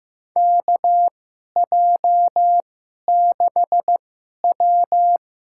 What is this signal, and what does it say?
Morse code, heard as a keyed tone, about 15 wpm, spaced as for 11 wpm, 700 Hz. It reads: KJ6W
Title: USA Call Signs - 2x Character Spacing - 15wpm